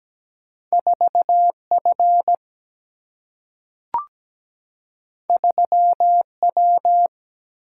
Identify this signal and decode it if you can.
Morse code, heard as a keyed tone, about 17 wpm, 700 Hz. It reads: 4 F 3 W
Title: Single-Letter Single-Number 17wpm